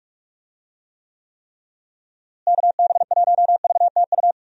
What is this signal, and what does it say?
Morse code, heard as a keyed tone, about 43 wpm, 700 Hz. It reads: KB1VTU
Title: USA Call Signs - 6 Characters - Repeated 3x - 45wpm